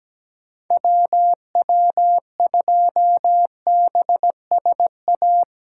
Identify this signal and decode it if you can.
Morse code, heard as a keyed tone, about 17 wpm, 700 Hz. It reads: WW2BSA
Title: USA Callsign DE Callsign - 6 Characters - 17wpm